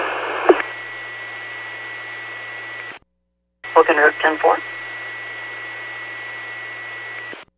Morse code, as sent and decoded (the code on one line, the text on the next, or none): none